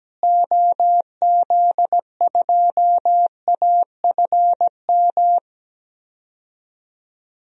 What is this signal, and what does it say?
Morse code, heard as a keyed tone, about 17 wpm, 700 Hz. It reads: OZ2AFM
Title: DXCC-100 Call Signs - 6 Characters - Repeated 2x - 17wpm